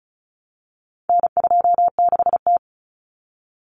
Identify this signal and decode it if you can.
Morse code, heard as a keyed tone, about 34 wpm, 700 Hz. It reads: N26T
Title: Common Cut Numbers - 4-Digit 35wpm